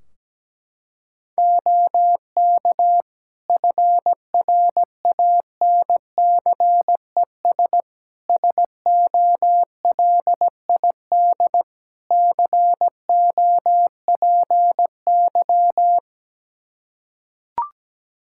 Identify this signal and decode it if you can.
Morse code, heard as a keyed tone, about 17 wpm, 700 Hz. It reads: OK FRANCES SOLID COPY
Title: Practice QSO - Exchange 3 - More Info - 17wpm